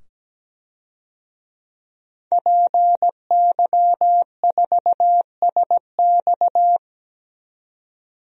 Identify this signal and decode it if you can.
Morse code, heard as a keyed tone, about 17 wpm, 700 Hz. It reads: PY4SX
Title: DXCC-100 Call Signs - 5 Characters - 17wpm